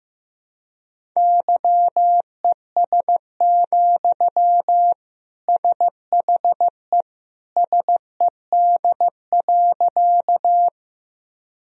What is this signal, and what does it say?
Morse code, heard as a keyed tone, about 15 wpm, 700 Hz. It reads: YES, SHE SED.
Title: Sentences from Top 500 Words and Abbreviations 15wpm